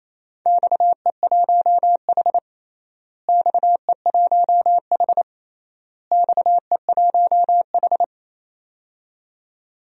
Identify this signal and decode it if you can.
Morse code, heard as a keyed tone, about 28 wpm, 700 Hz. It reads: XE1H XE1H XE1H
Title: DXCC-100 Call Signs - 4 Characters - Repeated 3x - 28wpm